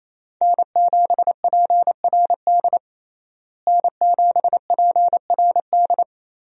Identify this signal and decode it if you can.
Morse code, heard as a keyed tone, about 28 wpm, 700 Hz. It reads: N7PRD N7PRD
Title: USA Call Signs - 5 Characters - Repeated 2x - 28wpm